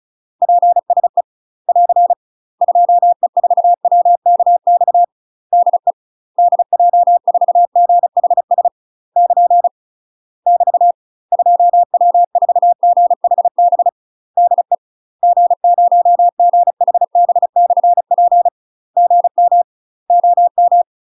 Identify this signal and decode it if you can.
Morse code, heard as a keyed tone, about 36 wpm, 700 Hz. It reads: PSE <AR> 2E4WKX DE DJ4GHS <KN> <BT> 2W4GHB DE G0GHB/P GM OM
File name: UK MASTER QSOs TRAINING / 35WPM / UK MASTER QSOs 0010